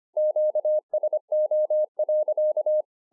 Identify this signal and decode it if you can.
Morse code, heard as a keyed tone, about 26 wpm, 610 Hz. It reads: QSO.
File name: 2011-04-mp3 / BEGINNERS DIARY 25WPM